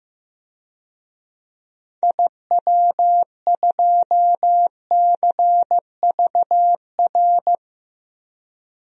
Morse code, heard as a keyed tone, about 15 wpm, 700 Hz. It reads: IW2CVR
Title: DXCC-100 Call Signs - 6 Characters - 15wpm